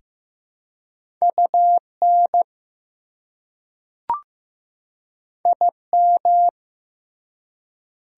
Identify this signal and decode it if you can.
Morse code, heard as a keyed tone, about 15 wpm, 700 Hz. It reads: UN IM
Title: Prefixes and Suffixes 15wpm